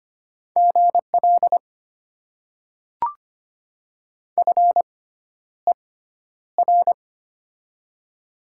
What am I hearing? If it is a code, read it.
Morse code, heard as a keyed tone, about 25 wpm, spaced as for 11 wpm, 700 Hz. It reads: GL FER
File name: Top 90 QSO Elements - 6x Character Spacing - 25wpm